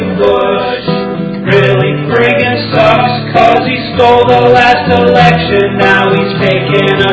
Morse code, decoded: none

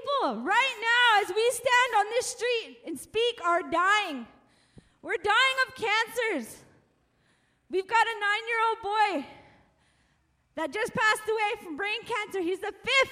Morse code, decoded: none

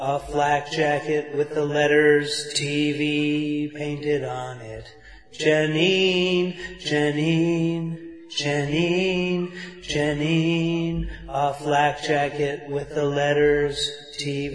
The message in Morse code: none

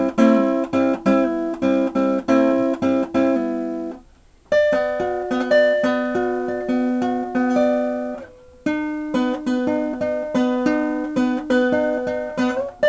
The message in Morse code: none

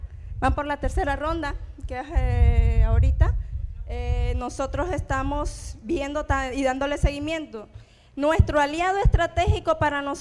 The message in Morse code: none

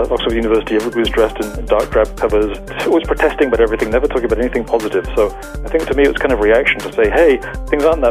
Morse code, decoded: none

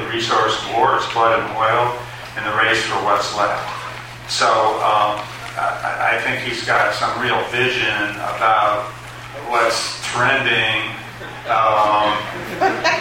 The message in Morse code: none

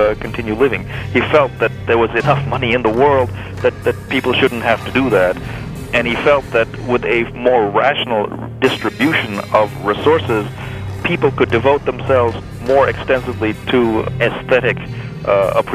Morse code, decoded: none

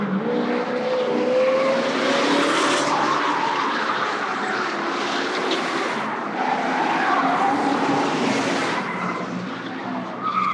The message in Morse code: none

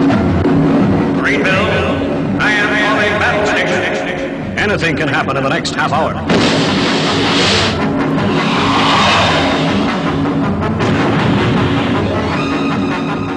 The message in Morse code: none

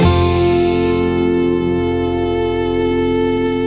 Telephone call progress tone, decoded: dial tone

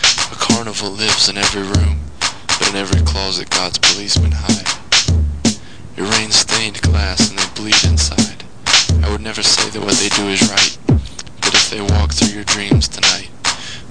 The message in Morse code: none

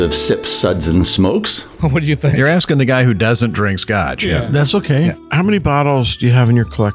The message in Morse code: none

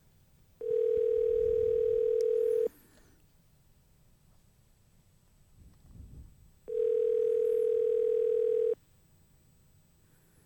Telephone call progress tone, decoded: ringback tone

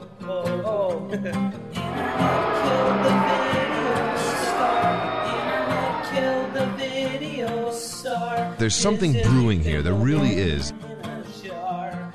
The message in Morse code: none

